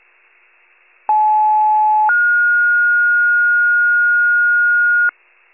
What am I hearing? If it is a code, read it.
Two-tone page, heard as a one-second tone, then a three-second tone, about 860 then 1430 Hz, with noise. It is heard from 1.1 to 5.1 s.